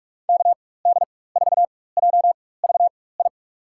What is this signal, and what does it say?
Morse code, heard as a keyed tone, about 47 wpm, spaced as for 21 wpm, 700 Hz. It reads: KD4JVI
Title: USA Call Signs - 6 Characters - 4x Character Spacing - 45wpm